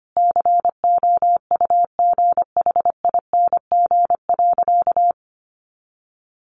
Morse code, heard as a keyed tone, about 25 wpm, 700 Hz. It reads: COUGHING.
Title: Additional Easy Phrases 25wpm